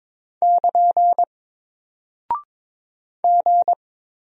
Morse code, heard as a keyed tone, about 22 wpm, 700 Hz. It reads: <KN> G
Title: Single Character (Letter:Number:Punctuation:Prosign) Alt 22wpm